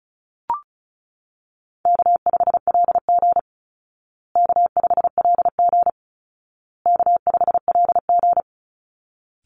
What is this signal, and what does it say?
Morse code, heard as a keyed tone, about 34 wpm, 700 Hz. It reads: K5LG K5LG K5LG
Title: Easy Call Signs Repeated 3x - 35wpm